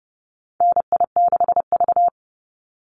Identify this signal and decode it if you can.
Morse code, heard as a keyed tone, about 30 wpm, 700 Hz. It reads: NI6V
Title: Call Signs - 30wpm